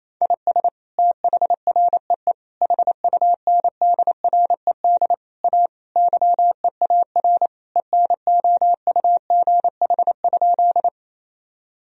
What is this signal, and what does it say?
Morse code, heard as a keyed tone, about 28 wpm, 700 Hz. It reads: IS THREE HUNDRED A YEAR ENOUGH?
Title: Sentences from Top 1000 Words - Review Part 2 - 28wpm